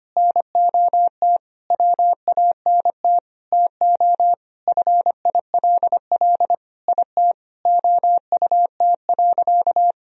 Morse code, heard as a keyed tone, about 25 wpm, 700 Hz. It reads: NOT WANT TO FILL IT OUT.
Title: Sentences from Top 1000 Words - Review Part 3 - 25wpm